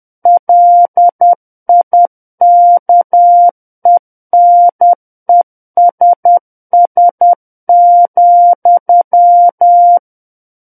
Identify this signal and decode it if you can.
Morse code, heard as a keyed tone, about 10 wpm, 700 Hz. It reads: LIKENESS,
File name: Book One / 01-Chapter 1 - 10 WPM